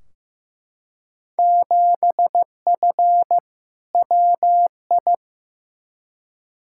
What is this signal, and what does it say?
Morse code, heard as a keyed tone, about 15 wpm, 700 Hz. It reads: 7F WI
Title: Field Day Exchanges Encore - 15wpm